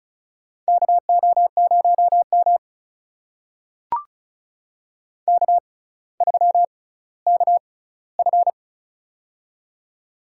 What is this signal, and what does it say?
Morse code, heard as a keyed tone, about 36 wpm, spaced as for 18 wpm, 700 Hz. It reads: KO0M K3KF
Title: USA Call Signs - 4 Characters - 6x Character Spacing - 35wpm